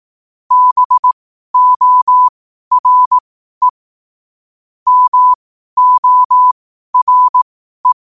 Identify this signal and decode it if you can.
Morse code, heard as a keyed tone, about 18 wpm, spaced as for 12 wpm, 1000 Hz. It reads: BORE MORE
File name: qotdfiles / 12wpm